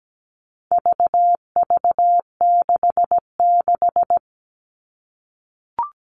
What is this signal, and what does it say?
Morse code, heard as a keyed tone, about 17 wpm, 700 Hz. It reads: VV66
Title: All Cut Numbers - 4-Digit 17wpm